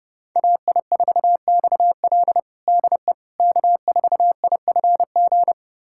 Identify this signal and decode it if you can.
Morse code, heard as a keyed tone, about 32 wpm, 700 Hz. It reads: AI4XL DE K4IFG
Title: USA Callsign DE Callsign - 5 Characters - 30wpm